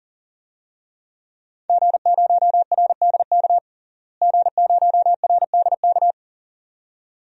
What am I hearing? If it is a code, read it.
Morse code, heard as a keyed tone, about 40 wpm, 700 Hz. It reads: G0RDK G0RDK
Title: DXCC-100 Call Signs - 5 Characters - Repeated 2x - 40wpm